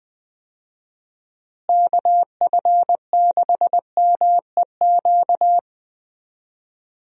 Morse code, heard as a keyed tone, about 20 wpm, 700 Hz. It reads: KF6MEQ
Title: USA Call Signs - 6 Characters - Repeated 2x - 20wpm